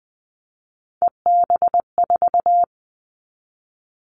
Morse code, heard as a keyed tone, about 20 wpm, 700 Hz. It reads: EB4
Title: Callsign Prefixes for Top 25 DXCC - 20wpm